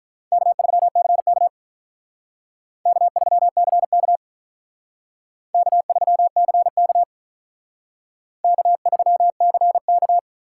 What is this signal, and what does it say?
Morse code, heard as a keyed tone, about 43 wpm, 700 Hz. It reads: K3CK K3CK K3CK K3CK
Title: Speed Racing - USA 1x2 Call Signs 35wpm